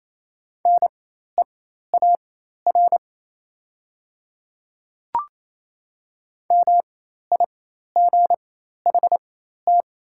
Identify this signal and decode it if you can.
Morse code, heard as a keyed tone, about 28 wpm, spaced as for 13 wpm, 700 Hz. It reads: NEAR MIGHT
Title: 4x Character Spacing - Top 500 Words - 28wpm